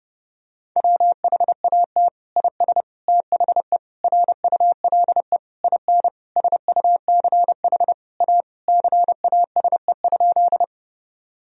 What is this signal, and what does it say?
Morse code, heard as a keyed tone, about 32 wpm, 700 Hz. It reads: WHAT IS THE RULE IN SUCH A CASE?
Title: Sentences from Top 1000 Words - Review Part 2 - 30wpm